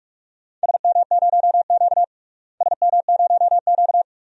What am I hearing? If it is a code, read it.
Morse code, heard as a keyed tone, about 43 wpm, 700 Hz. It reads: SM0Q SM0Q
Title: DXCC-100 Call Signs - 4 Characters - Repeated 2x - 45wpm